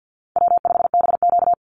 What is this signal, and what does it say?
Morse code, heard as a keyed tone, about 50 wpm, 700 Hz. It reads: W5BQ
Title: Call Signs Repeated 3x - 50wpm